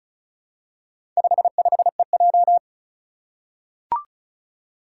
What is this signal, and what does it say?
Morse code, heard as a keyed tone, about 36 wpm, 700 Hz. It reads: 55EJ
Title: Four Challenging Letter-Number Combinations 35wpm